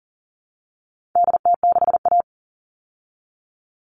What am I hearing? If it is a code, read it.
Morse code, heard as a keyed tone, about 40 wpm, 700 Hz. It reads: DT6A
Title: All Cut Numbers - 4-Digit 40wpm